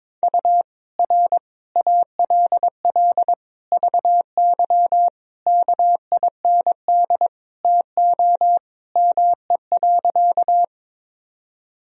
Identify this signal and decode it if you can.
Morse code, heard as a keyed tone, about 21 wpm, 700 Hz. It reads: U R ALL VY KIND TO ME.
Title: Sentences from Top 200 Words and Abbreviations 22wpm